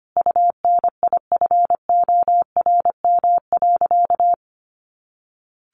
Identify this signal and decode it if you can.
Morse code, heard as a keyed tone, about 25 wpm, 700 Hz. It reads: UNIFORM.
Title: Easy Phrases 25wpm